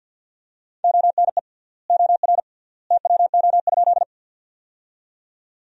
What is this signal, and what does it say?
Morse code, heard as a keyed tone, about 50 wpm, 700 Hz. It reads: ONE OR TWO?
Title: 3x Word-Spacing - Sentences from Top 100 Words - 50wpm